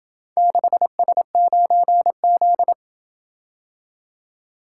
Morse code, heard as a keyed tone, about 27 wpm, 700 Hz. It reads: 6S9Z